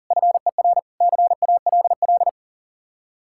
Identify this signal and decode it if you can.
Morse code, heard as a keyed tone, about 52 wpm, 700 Hz. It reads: FER CALL